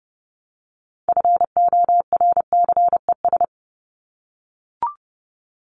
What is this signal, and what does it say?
Morse code, heard as a keyed tone, about 30 wpm, 700 Hz. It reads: FORCES
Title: ES:S Suffix - Top 100 Words with Base Words 30wpm